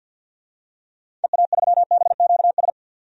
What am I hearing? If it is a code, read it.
Morse code, heard as a keyed tone, about 51 wpm, 700 Hz. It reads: EA3BQS
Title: DXCC-100 Call Signs - 6 Characters - 50wpm